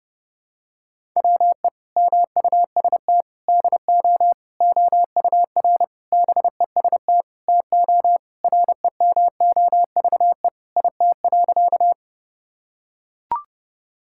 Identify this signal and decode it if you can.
Morse code, heard as a keyed tone, about 29 wpm, 700 Hz. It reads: WE MUST DO OUR BEST TO REMOVE IT.
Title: Sentences from Top 1000 Words - Review Part 5 - 30wpm